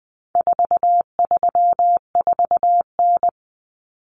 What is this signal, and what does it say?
Morse code, heard as a keyed tone, about 20 wpm, 700 Hz. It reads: 434N
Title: Common Cut Numbers - 4-Digit 20wpm